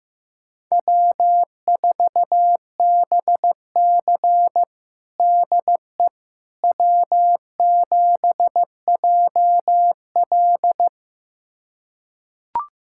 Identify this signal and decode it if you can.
Morse code, heard as a keyed tone, about 15 wpm, 700 Hz. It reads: W4BC DE W7JL
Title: USA Callsign DE Callsign - 4 Characters - 15wpm